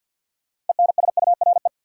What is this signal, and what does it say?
Morse code, heard as a keyed tone, about 52 wpm, 700 Hz. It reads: ENSURE
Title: Sentences from Top 1100 Words - New Word Drill - 50wpm